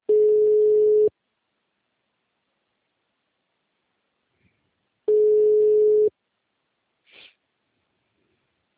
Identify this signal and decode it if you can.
Ringback tone (Europe, 425 Hz).